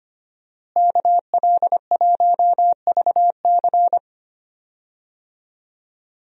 Morse code, heard as a keyed tone, about 25 wpm, 700 Hz. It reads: KL1VC
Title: USA Call Signs - 5 Characters - Repeated 2x - 25wpm